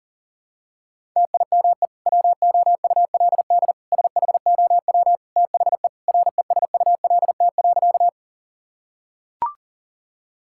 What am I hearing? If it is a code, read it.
Morse code, heard as a keyed tone, about 40 wpm, 700 Hz. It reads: TIME WOULD SHOW THE RESULT.
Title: Sentences from Top 1000 Words - Review Part 5 - 40wpm